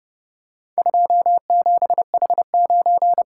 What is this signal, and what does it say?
Morse code, heard as a keyed tone, about 29 wpm, 700 Hz. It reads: 27H9